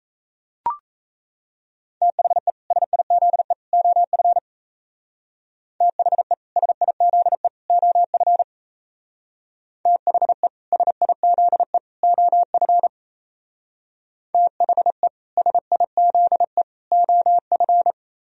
Morse code, heard as a keyed tone, about 35 wpm, 700 Hz. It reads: THE SIZE OF THE SIZE OF THE SIZE OF THE SIZE OF
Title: Speed-Racing - Sets of 3 Words 28wpm